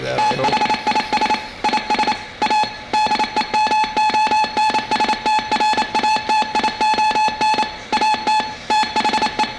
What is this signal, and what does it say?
Morse code, heard as a keyed tone, about 28 wpm, 860 Hz. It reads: THIS IS A DEMONSTRATION AT THE